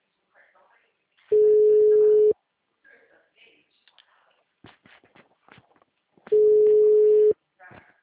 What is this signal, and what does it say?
Ringback tone (Europe, 425 Hz), with speech.